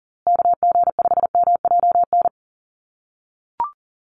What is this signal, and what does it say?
Morse code, heard as a keyed tone, about 40 wpm, 700 Hz. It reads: KG5MJN
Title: USA Call Signs Repeated 2x 40wpm